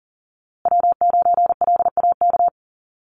Morse code, heard as a keyed tone, about 40 wpm, 700 Hz. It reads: W9LAK
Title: USA Call Signs Repeated 3x 40wpm